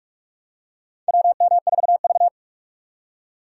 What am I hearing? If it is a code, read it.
Morse code, heard as a keyed tone, about 43 wpm, 700 Hz. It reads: WM4V
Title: USA Call Signs - 4 Characters - 4x Character Spacing - 45wpm